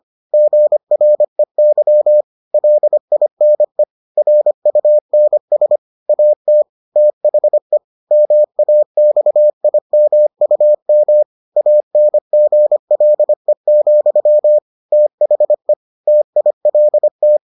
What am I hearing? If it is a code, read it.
Morse code, heard as a keyed tone, about 25 wpm, 600 Hz. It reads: GREY LINE RUNS AT THE MAXIMUM ANGLE, THE TILT